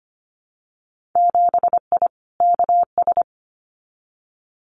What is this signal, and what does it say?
Morse code, heard as a keyed tone, about 25 wpm, 700 Hz. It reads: 7I KS
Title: Winter Field Day Exchanges Encore 25wpm